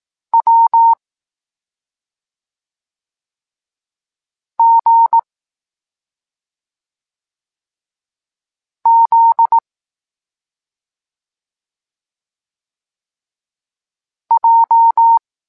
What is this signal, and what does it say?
Morse code, heard as a keyed tone, about 18 wpm, 910 Hz. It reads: W G Z J